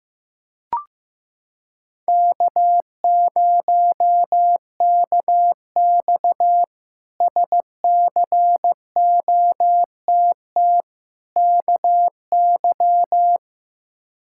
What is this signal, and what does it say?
Morse code, heard as a keyed tone, about 16 wpm, 700 Hz. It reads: K0KX SCOTT KY
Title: K1USN SST QSO Practice - 15wpm